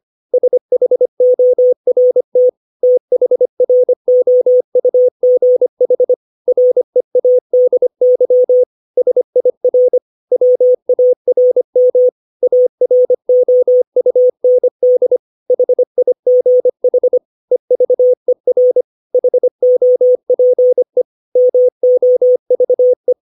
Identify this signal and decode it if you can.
Morse code, heard as a keyed tone, about 25 wpm, 500 Hz. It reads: SHORT THROUGH READY SIR WARM AROUND HIGH EVER HOPE MOVE
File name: 632 words / 632 Words 25WPM